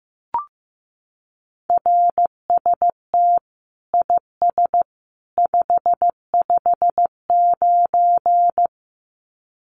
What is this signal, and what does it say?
Morse code, heard as a keyed tone, about 15 wpm, 700 Hz. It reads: RST IS 559
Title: Common QSO Phrases 15wpm